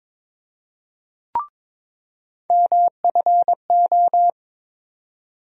Morse code, heard as a keyed tone, about 22 wpm, 700 Hz. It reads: MFO
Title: USA Call Sign Postfixes - 3 Letters 22wpm